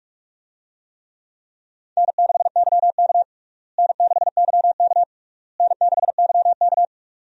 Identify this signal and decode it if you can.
Morse code, heard as a keyed tone, about 46 wpm, 700 Hz. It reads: N6YK N6YK N6YK
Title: USA Call Signs - 4 Characters - Repeated 3x - 45wpm